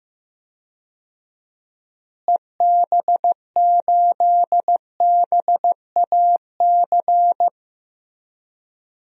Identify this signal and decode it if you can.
Morse code, heard as a keyed tone, about 15 wpm, 700 Hz. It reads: EB8BAC
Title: DXCC-100 Call Signs - 6 Characters - Repeated 2x - 15wpm